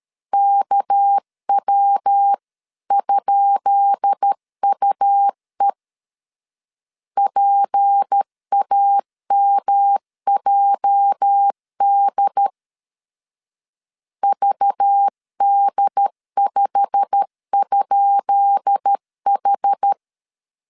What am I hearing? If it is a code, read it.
Morse code, heard as a keyed tone, about 13 wpm, 800 Hz. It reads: KW?UE PAMJD VD5?H
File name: LEZIONI / 10 SIGNS / 10B INT 12wpm 2414